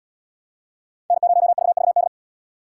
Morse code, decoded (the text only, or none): N0VVD